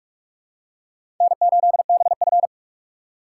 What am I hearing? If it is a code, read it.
Morse code, heard as a keyed tone, about 46 wpm, 700 Hz. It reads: N8BF
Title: USA Call Signs - 4 Characters - 2x Character Spacing - 45wpm